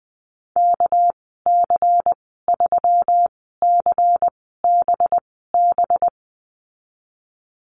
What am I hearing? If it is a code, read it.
Morse code, heard as a keyed tone, about 20 wpm, spaced as for 14 wpm, 700 Hz. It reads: KC3CBB
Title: 207 Lesson 33g - Letter j - Call Signs 20wpm